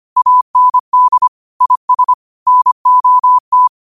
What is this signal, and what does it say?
Morse code, heard as a keyed tone, about 25 wpm, 1000 Hz. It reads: AND IS NOT